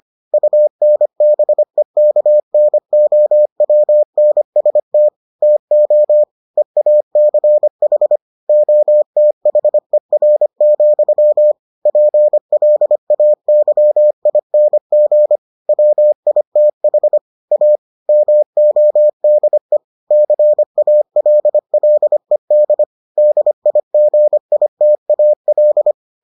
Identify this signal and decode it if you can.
Morse code, heard as a keyed tone, about 25 wpm, 600 Hz. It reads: UNBEKNOWNST TO EACH OTHER, PLAYING WITH A MODE CALLED DIGITAL